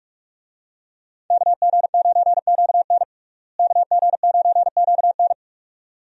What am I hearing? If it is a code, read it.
Morse code, heard as a keyed tone, about 43 wpm, 700 Hz. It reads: KG9QN KG9QN
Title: USA Call Signs - 5 Characters - Repeated 2x - 45wpm